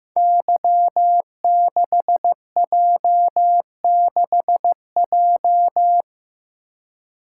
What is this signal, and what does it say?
Morse code, heard as a keyed tone, about 15 wpm, 700 Hz. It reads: Y6J6J
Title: Code Groups - 5 Letter:Numbers - 15wpm